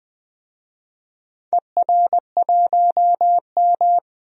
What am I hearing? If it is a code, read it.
Morse code, heard as a keyed tone, about 20 wpm, 700 Hz. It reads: ER1M
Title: DXCC-100 Call Signs - 4 Characters - 20wpm